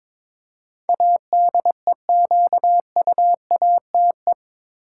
Morse code, heard as a keyed tone, about 22 wpm, 700 Hz. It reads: ADEQUATE
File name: practice / 22wpm / Eight Letter Words - 22wpm